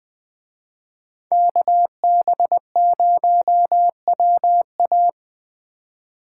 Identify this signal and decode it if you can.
Morse code, heard as a keyed tone, about 20 wpm, 700 Hz. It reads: KB0WA
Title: Speed Racing - USA 2x2 Call Signs 17wpm